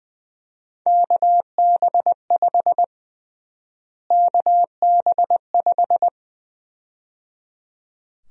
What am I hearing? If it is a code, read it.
Morse code, heard as a keyed tone, about 20 wpm, 700 Hz. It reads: KB5 KB5
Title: USA Call Sign Prefixes - Repeated 2x - 20wpm